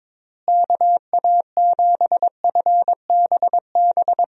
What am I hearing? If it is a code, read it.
Morse code, heard as a keyed tone, about 22 wpm, 700 Hz. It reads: KA7FBB
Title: USA Call Signs - 6 Characters - Repeated 2x - 22wpm